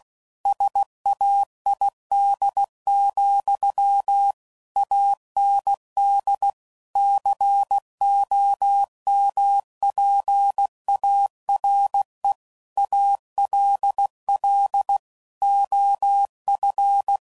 Morse code, heard as a keyed tone, about 16 wpm, 800 Hz. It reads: SAID, AND COMPARE ALL OF
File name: CW Code Groups @15WPM / Technical Subjects / OR Gate